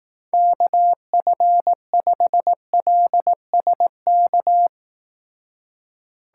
Morse code, heard as a keyed tone, about 18 wpm, 700 Hz. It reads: KF5LSK
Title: Speed Racing - USA 2x3 Call Signs 15wpm